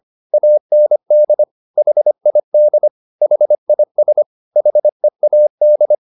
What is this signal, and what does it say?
Morse code, heard as a keyed tone, about 25 wpm, 600 Hz. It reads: AND HID HIS HEAD